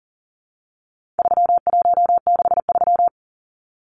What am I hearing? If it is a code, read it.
Morse code, heard as a keyed tone, about 40 wpm, 700 Hz. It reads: 3163
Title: Numbers - 4-Digit 40wpm